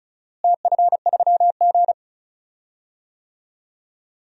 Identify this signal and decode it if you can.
Morse code, heard as a keyed tone, about 34 wpm, 700 Hz. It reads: TF3G
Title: DXCC-100 Call Signs - 4 Characters - Repeated 2x - 35wpm